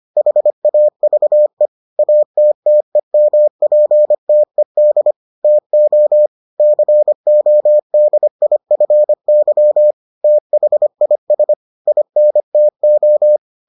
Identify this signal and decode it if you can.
Morse code, heard as a keyed tone, about 25 wpm, 600 Hz. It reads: HAVE ATTEMPTED TO CODIFY THIS INTO